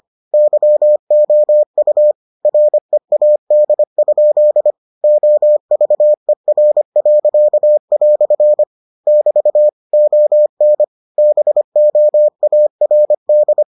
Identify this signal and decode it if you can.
Morse code, heard as a keyed tone, about 25 wpm, 600 Hz. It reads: YOU READ? OVER." <BT> ON BOARD